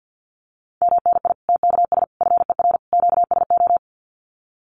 Morse code, heard as a keyed tone, about 48 wpm, 700 Hz. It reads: MNI TKS FER QSO
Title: Common QSO Phrases 50wpm